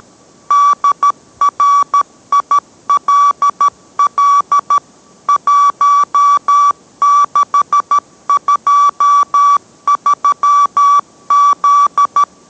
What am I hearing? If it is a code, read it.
Morse code, heard as a keyed tone, about 13 wpm, 1150 Hz. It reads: DRILL 1623Z